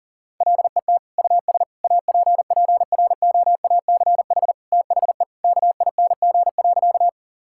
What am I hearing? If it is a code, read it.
Morse code, heard as a keyed tone, about 40 wpm, 700 Hz. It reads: LET US APPROACH THE KING.
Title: Sentences from Top 1000 Words - Review Part 6 - 40wpm